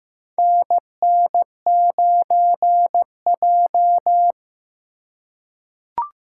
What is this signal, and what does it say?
Morse code, heard as a keyed tone, about 15 wpm, 700 Hz. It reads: NN9J
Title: USA Call Signs - 4 Characters - 4x Character Spacing - 15wpm